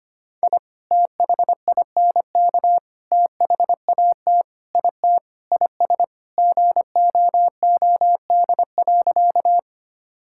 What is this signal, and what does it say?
Morse code, heard as a keyed tone, about 25 wpm, 700 Hz. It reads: I THINK THAT IT IS GOOD.
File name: practice / 25wpm / Sentences with 3-Word Phrases - 25wpm